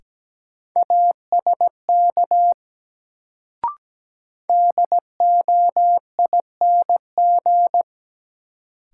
Morse code, heard as a keyed tone, about 17 wpm, 700 Hz. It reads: ASK DOING
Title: Sets of 4 Words - New Word Drill - 17wpm